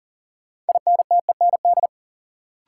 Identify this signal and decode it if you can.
Morse code, heard as a keyed tone, about 40 wpm, 700 Hz. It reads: INTEND
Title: Sentences from Top 1600 Words - New Word Drill - 40wpm